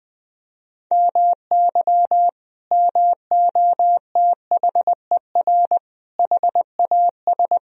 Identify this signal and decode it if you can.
Morse code, heard as a keyed tone, about 21 wpm, 700 Hz. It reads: MY MOTHER HAS
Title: Sentences from Top 1000 Words - Review Part 4 - 20wpm